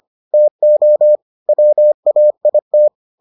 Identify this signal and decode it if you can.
Morse code, heard as a keyed tone, about 25 wpm, 600 Hz. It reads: TO WAIT